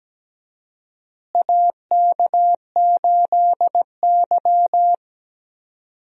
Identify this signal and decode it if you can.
Morse code, heard as a keyed tone, about 17 wpm, 700 Hz. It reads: AK8Y
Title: USA Call Signs - 4 Characters - Repeated 2x - 17wpm